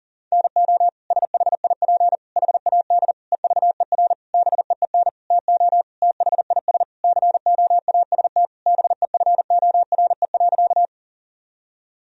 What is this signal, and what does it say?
Morse code, heard as a keyed tone, about 40 wpm, 700 Hz. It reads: NO SHIP HAD EVER BEEN TO THIS COAST BEFORE.
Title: Sentences from Top 1000 Words - Review Part 4 - 40wpm